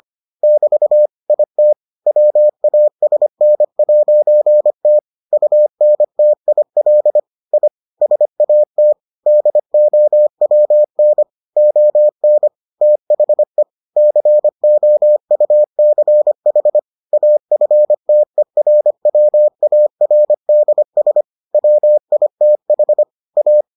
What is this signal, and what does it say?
Morse code, heard as a keyed tone, about 25 wpm, 600 Hz. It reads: <BT> IT WASN'T UNTIL I SAT DOWN ON THE COUCH AFTERWARDS WITH A